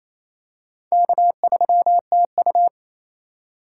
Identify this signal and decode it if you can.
Morse code, heard as a keyed tone, about 28 wpm, 700 Hz. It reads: K3TU
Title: USA Call Signs - 4 Characters - 2x Character Spacing - 28wpm